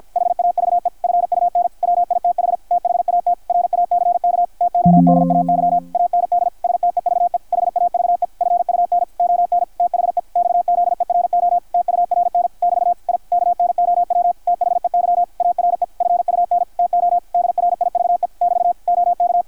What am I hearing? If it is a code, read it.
Morse code, heard as a keyed tone, about 53 wpm, 660 Hz, with radio noise. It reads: HAVE FUN WITH THAT RACK TODAY AND STEVE HAVE FUN ON THE XZERO TURN <BT> I KNOW THEY ARE FUN TO DRIVE <BT> OK
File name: archived / CRSnet